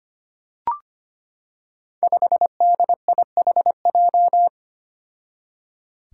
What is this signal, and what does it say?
Morse code, heard as a keyed tone, about 25 wpm, 700 Hz. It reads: 5DIHJ
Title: Five Challenging Letter-Number Combinations 25wpm